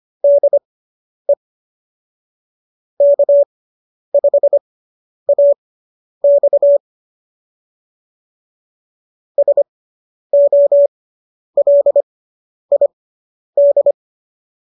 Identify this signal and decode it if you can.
Morse code, heard as a keyed tone, about 25 wpm, spaced as for 10 wpm, 570 Hz. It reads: DE K5AX SOLID